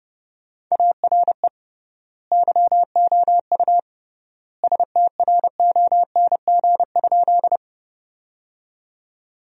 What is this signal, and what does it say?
Morse code, heard as a keyed tone, about 30 wpm, 700 Hz. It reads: ARE YOU STRONG?
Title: 3x Word-Spacing - Sentences from Top 500 Words - 30wpm